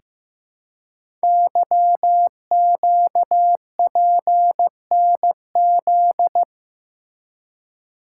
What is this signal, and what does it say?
Morse code, heard as a keyed tone, about 15 wpm, 700 Hz. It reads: YQPNZ
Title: Code Groups - 5 Letters - 15wpm